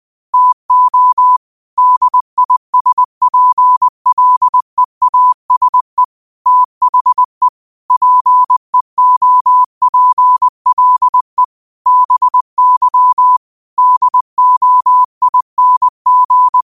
Morse code, heard as a keyed tone, about 20 wpm, 1000 Hz. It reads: TO DISPLEASE THE PEOPLE BY DOING
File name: qotdfiles / 20wpm